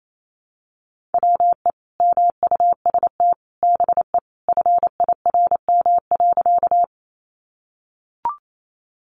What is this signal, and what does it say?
Morse code, heard as a keyed tone, about 28 wpm, 700 Hz. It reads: WE MUST BE FIRM.
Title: Sentences from Top 1400 Words 28wpm